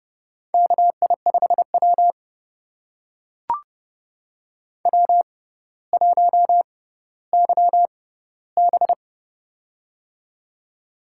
Morse code, heard as a keyed tone, about 30 wpm, spaced as for 15 wpm, 700 Hz. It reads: KI5W W1YB
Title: USA Call Signs - 6x Character Spacing - 30wpm